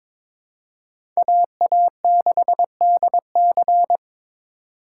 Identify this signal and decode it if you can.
Morse code, heard as a keyed tone, about 22 wpm, 700 Hz. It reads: AA6DC
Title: USA Call Signs - 5 Characters - 3x Character Spacing - 22wpm